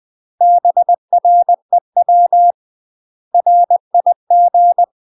Morse code, heard as a keyed tone, about 20 wpm, 700 Hz. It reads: BREW RIG
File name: UK MASTER QSOs TRAINING / 20WPM / UK MASTER QSOs 0024